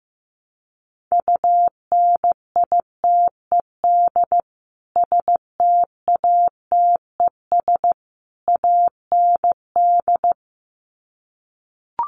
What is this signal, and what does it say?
Morse code, heard as a keyed tone, about 15 wpm, 700 Hz. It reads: UNITED STATES AND
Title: Sets of 3 Words 15wpm